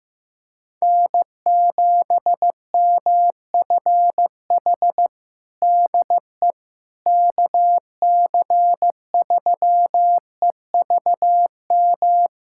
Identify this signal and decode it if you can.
Morse code, heard as a keyed tone, about 16 wpm, 700 Hz. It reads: N7MFH DE KC3EVM
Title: USA Callsign DE Callsign 15wpm